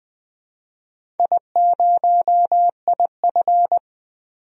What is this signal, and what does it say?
Morse code, heard as a keyed tone, about 20 wpm, 700 Hz. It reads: I0IF
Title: DXCC-100 Call Signs - 4 Characters - 20wpm